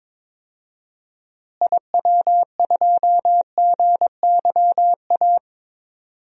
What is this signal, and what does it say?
Morse code, heard as a keyed tone, about 22 wpm, 700 Hz. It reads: IW2GYA
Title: DXCC-100 Call Signs - 6 Characters - Repeated 2x - 22wpm